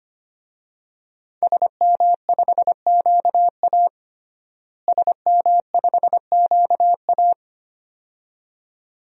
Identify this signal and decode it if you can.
Morse code, heard as a keyed tone, about 25 wpm, 700 Hz. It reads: SM5QA SM5QA
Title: DXCC-100 Call Signs - 5 Characters - Repeated 2x - 25wpm